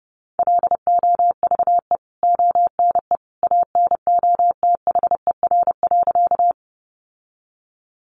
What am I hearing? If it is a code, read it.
Morse code, heard as a keyed tone, about 30 wpm, 700 Hz. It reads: LOVE ONE ANOTHER.
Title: Sentences from Top 600 Words 30wpm